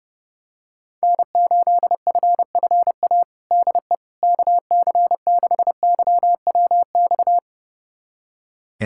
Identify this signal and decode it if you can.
Morse code, heard as a keyed tone, about 30 wpm, 700 Hz. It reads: N8FFA DE KC6YWX
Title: USA Callsign DE Callsign 30wpm